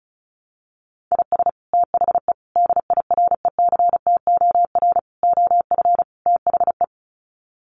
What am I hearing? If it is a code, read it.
Morse code, heard as a keyed tone, about 34 wpm, 700 Hz. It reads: IS THE DIRECTOR OF THE
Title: Sets of 5 Words Encore2 35wpm